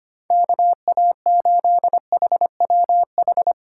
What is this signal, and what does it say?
Morse code, heard as a keyed tone, about 31 wpm, 700 Hz. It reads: KA8HWH
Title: Speed Racing - USA 2x3 Call Signs 25wpm